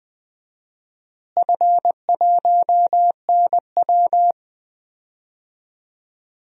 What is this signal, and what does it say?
Morse code, heard as a keyed tone, about 21 wpm, 700 Hz. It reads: F1NW
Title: DXCC-100 Call Signs - 4 Characters - Repeated 2x - 20wpm